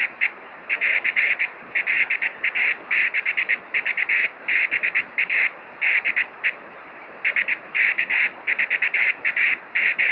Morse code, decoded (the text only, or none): none